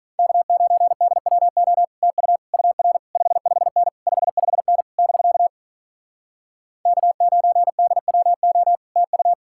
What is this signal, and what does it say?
Morse code, heard as a keyed tone, about 47 wpm, 700 Hz. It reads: K9DWO TU UR 55N 55N <BK> K9DWO TU